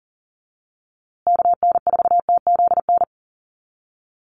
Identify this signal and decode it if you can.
Morse code, heard as a keyed tone, about 40 wpm, 700 Hz. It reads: KN4TZN
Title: USA Call Signs Repeated 2x 40wpm